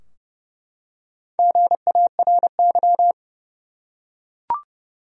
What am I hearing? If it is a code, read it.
Morse code, heard as a keyed tone, about 31 wpm, 700 Hz. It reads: GARY
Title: USA Common Names - 30wpm